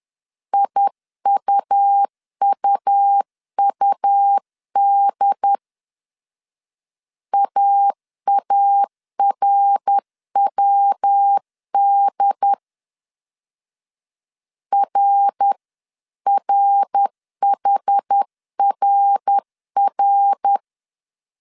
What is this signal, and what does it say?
Morse code, heard as a keyed tone, about 10 wpm, 800 Hz. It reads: IUUUD AARWD RRHRR
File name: 10wpm